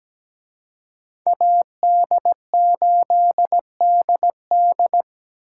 Morse code, heard as a keyed tone, about 20 wpm, 700 Hz. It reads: AD8DD
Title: Speed Racing - USA 2x2 Call Signs 17wpm